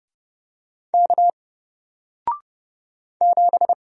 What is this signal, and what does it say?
Morse code, heard as a keyed tone, about 30 wpm, 700 Hz. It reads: K 7
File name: Single-Letter-Number-30wpm